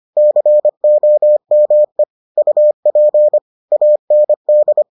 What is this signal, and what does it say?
Morse code, heard as a keyed tone, about 25 wpm, 600 Hz. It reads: COME UP AND